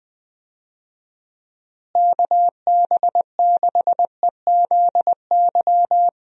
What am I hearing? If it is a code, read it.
Morse code, heard as a keyed tone, about 21 wpm, 700 Hz. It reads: KB6EZY
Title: USA Call Signs - 6 Characters - Repeated 3x - 20wpm